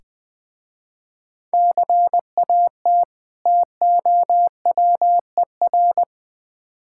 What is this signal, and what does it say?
Morse code, heard as a keyed tone, about 20 wpm, 700 Hz. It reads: CAT TOWER
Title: Difficult Words with Hints - 20wpm